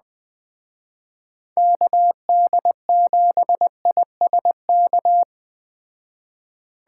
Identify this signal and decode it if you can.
Morse code, heard as a keyed tone, about 20 wpm, 700 Hz. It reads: KD7ISK